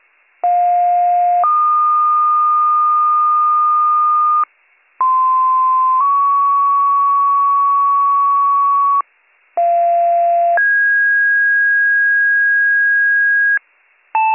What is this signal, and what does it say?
Two-tone page, heard as a one-second tone, then a three-second tone, about 710 then 1220 Hz, about 1040 then 1120 Hz, about 690 then 1650 Hz. It is heard from 0.4 to 13.6 s.